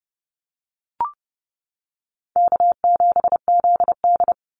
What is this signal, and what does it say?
Morse code, heard as a keyed tone, about 30 wpm, 700 Hz. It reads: K7ZD
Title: Easy Call Signs - 30wpm